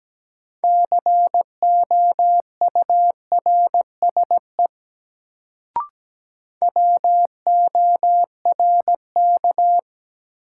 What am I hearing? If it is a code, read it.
Morse code, heard as a keyed tone, about 17 wpm, 700 Hz. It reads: COURSE WORK
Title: Sets of 2 Words - New Word Drill - 17wpm